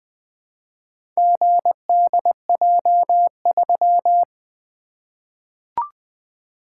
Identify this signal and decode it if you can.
Morse code, heard as a keyed tone, about 20 wpm, 700 Hz. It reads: GDJ3
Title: Four Letter-Number Combinations - 20wpm